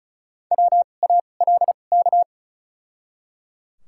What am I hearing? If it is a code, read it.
Morse code, heard as a keyed tone, about 36 wpm, 700 Hz. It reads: W A L K
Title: 2x Character Spacing - Top 500 Words - 35wpm